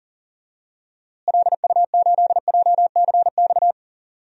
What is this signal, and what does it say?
Morse code, heard as a keyed tone, about 38 wpm, 700 Hz. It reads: LU8JCX
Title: DXCC-100 Call Signs - 6 Characters - Repeated 3x - 40wpm